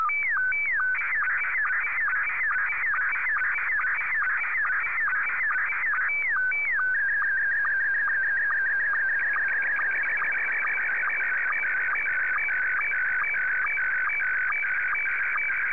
SSTV (slow-scan television), Scottie 1.